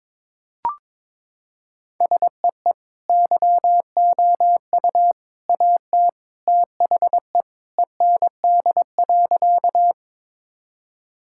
Morse code, heard as a keyed tone, about 22 wpm, 700 Hz. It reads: SEE YOU AT THE END.
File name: practice / 22wpm / Sentences with 3-Word Phrases - 22wpm